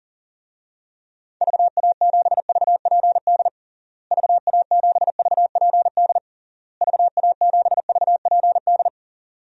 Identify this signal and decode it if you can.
Morse code, heard as a keyed tone, about 40 wpm, 700 Hz. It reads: VA7VPD VA7VPD VA7VPD
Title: DXCC-100 Call Signs - 6 Characters - Repeated 3x - 40wpm